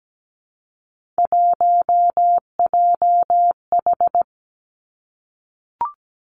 Morse code, heard as a keyed tone, about 17 wpm, 700 Hz. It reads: 1JH
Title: Three Challenging Letter-Number Combinations 17wpm